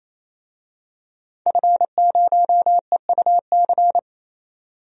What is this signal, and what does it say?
Morse code, heard as a keyed tone, about 28 wpm, 700 Hz. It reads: F0EUC
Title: DXCC-100 Call Signs - 5 Characters - 28wpm